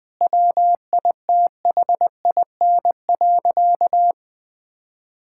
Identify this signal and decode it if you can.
Morse code, heard as a keyed tone, about 20 wpm, 700 Hz. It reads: WITHIN.
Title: Sentences from Top 1600 Words 20wpm